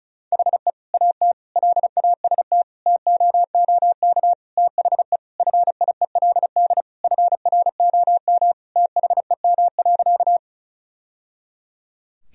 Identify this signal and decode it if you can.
Morse code, heard as a keyed tone, about 34 wpm, 700 Hz. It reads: HE AT LAST TOOK THE FIELD FROM THEM.
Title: Sentences from Top 1000 Words - Review Part 2 - 35wpm